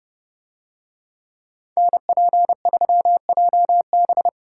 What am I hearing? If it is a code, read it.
Morse code, heard as a keyed tone, about 29 wpm, 700 Hz. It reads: NP3JB